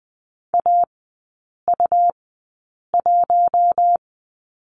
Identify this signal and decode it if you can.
Morse code, heard as a keyed tone, about 20 wpm, spaced as for 8 wpm, 700 Hz. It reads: AU1
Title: 100 Lesson 17e - Letter u - Three Characters 20wpm